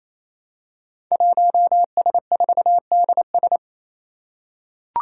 Morse code, heard as a keyed tone, about 27 wpm, 700 Hz. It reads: 1S4DS
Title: Five Challenging Letter-Number Combinations 28wpm